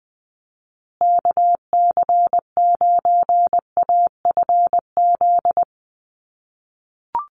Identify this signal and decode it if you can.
Morse code, heard as a keyed tone, about 20 wpm, 700 Hz. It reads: KC9AFZ
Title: USA Call Signs Repeated 3x 20wpm